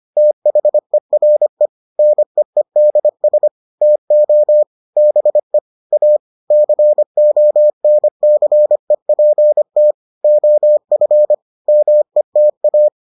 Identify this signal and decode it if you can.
Morse code, heard as a keyed tone, about 25 wpm, 600 Hz. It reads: THERE NEEDS TO BE A CONCEPT OF META